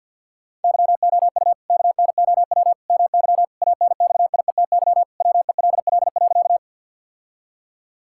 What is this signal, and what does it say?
Morse code, heard as a keyed tone, about 50 wpm, 700 Hz. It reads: YOU KNOW MY ANXIETY WELL.
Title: Sentences from Top 1600 Words 50wpm